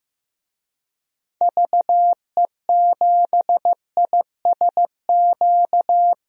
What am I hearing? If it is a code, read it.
Morse code, heard as a keyed tone, about 15 wpm, 700 Hz. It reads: VE7ISQ
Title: DXCC-100 Call Signs - 6 Characters - 15wpm